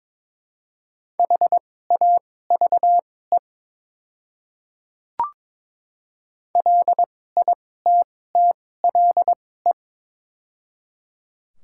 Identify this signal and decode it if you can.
Morse code, heard as a keyed tone, about 22 wpm, spaced as for 16 wpm, 700 Hz. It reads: HAVE LITTLE
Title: 2x Character Spacing - Top 200 Words - 22wpm